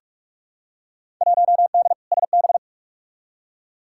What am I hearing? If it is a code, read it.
Morse code, heard as a keyed tone, about 45 wpm, 700 Hz. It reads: 1D SB
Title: Field Day Exchanges - 45wpm